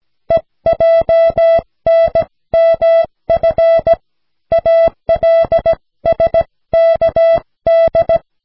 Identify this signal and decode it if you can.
Morse code, heard as a keyed tone, about 16 wpm, 660 Hz. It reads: EJNMF ALSKD
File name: cw / snelheid 4